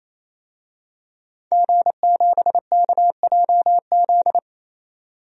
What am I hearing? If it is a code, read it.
Morse code, heard as a keyed tone, about 27 wpm, 700 Hz. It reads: G7KJZ
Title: DXCC-100 Call Signs - 5 Characters - 28wpm